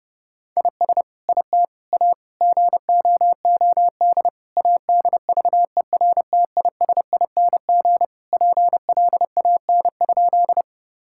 Morse code, heard as a keyed tone, about 30 wpm, 700 Hz. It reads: IS IT A GOOD ADVERTISING PLAN?
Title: Sentences from Top 1000 Words - Review Part 6 - 30wpm